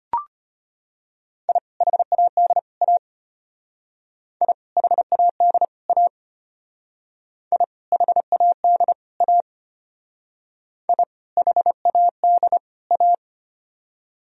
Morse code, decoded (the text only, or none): I HAD A I HAD A I HAD A I HAD A